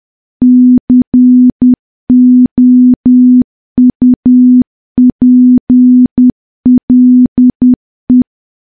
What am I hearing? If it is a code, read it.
Morse code, heard as a keyed tone, about 10 wpm, 250 Hz. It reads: COUPLE
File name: morse / morse-training-10WPM